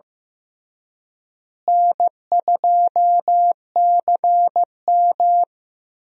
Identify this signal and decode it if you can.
Morse code, heard as a keyed tone, about 15 wpm, 700 Hz. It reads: N2CM
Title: USA Call Signs - 4 Characters - Repeated 3x - 15wpm